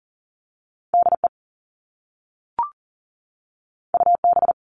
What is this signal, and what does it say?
Morse code, heard as a keyed tone, about 41 wpm, 700 Hz. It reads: DE UB
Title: All Cut Numbers - 2-Digit 40wpm